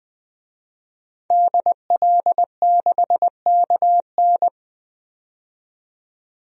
Morse code, heard as a keyed tone, about 20 wpm, 700 Hz. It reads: DL6KN
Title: DXCC-100 Call Signs - 5 Characters - 20wpm